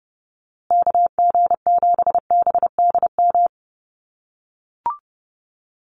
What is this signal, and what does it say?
Morse code, heard as a keyed tone, about 30 wpm, 700 Hz. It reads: KG7BDM
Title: USA Call Signs Repeated 2x 30wpm